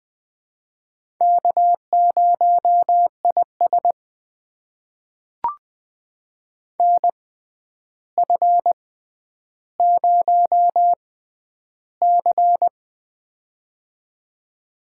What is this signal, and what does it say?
Morse code, heard as a keyed tone, about 20 wpm, spaced as for 10 wpm, 700 Hz. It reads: K0IS NF0C
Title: USA Call Signs - 4 Characters - 6x Character Spacing - 20wpm